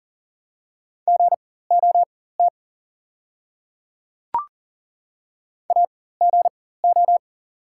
Morse code, heard as a keyed tone, about 40 wpm, spaced as for 19 wpm, 700 Hz. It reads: GOT AGO
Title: 4x Character Spacing - Top 500 Words - 40wpm